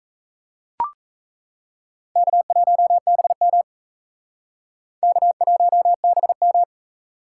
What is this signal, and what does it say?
Morse code, heard as a keyed tone, about 36 wpm, 700 Hz. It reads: K1BM K1BM